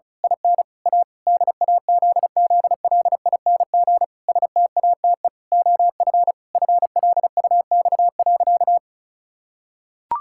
Morse code, heard as a keyed tone, about 34 wpm, 700 Hz. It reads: IN A DAZZLING STATE OF FLUX.
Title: Pangram Challenge v2 - 35wpm